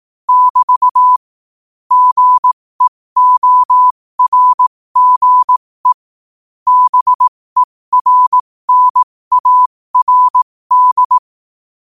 Morse code, heard as a keyed tone, about 18 wpm, spaced as for 15 wpm, 1000 Hz. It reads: <BT> GEORGE BERNARD
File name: qotdfiles / 15wpm